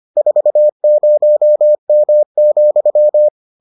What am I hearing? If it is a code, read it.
Morse code, heard as a keyed tone, about 25 wpm, 600 Hz. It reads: 40M,